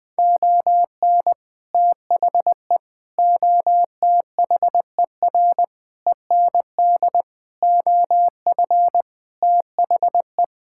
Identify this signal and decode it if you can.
Morse code, heard as a keyed tone, about 21 wpm, 700 Hz. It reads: ON THE OTHER END OF THE
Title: Sentences with 7-Word Phrases - Spoken Beginning - 20wpm